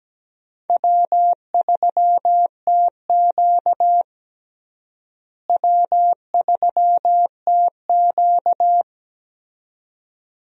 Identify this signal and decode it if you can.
Morse code, heard as a keyed tone, about 17 wpm, 700 Hz. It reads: W3TQ W3TQ
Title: USA Call Signs - 4 Characters - Repeated 2x - 17wpm